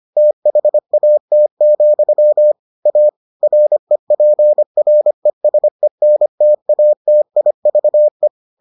Morse code, heard as a keyed tone, about 25 wpm, 600 Hz. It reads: THAT, A REPRESENTATIVE